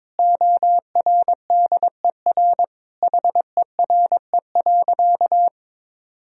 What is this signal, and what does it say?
Morse code, heard as a keyed tone, about 22 wpm, 700 Hz. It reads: ORDER HERE.
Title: Sentences from Top 1000 Words - Review Part 2 - 22wpm